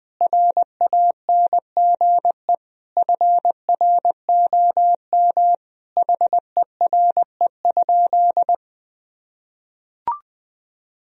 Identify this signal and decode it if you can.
Morse code, heard as a keyed tone, about 21 wpm, 700 Hz. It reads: RANGE FROM HERE?
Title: Sentences from Top 1000 Words - Review Part 4 - 20wpm